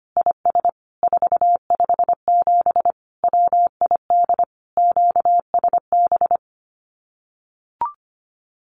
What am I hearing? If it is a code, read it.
Morse code, heard as a keyed tone, about 25 wpm, 700 Hz. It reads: IS 457 WID QSB
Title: Common QSO Phrases 25wpm